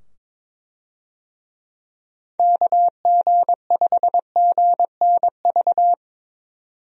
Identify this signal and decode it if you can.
Morse code, heard as a keyed tone, about 22 wpm, 700 Hz. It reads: KG5GNV
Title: USA Call Signs - 6 Characters - 22wpm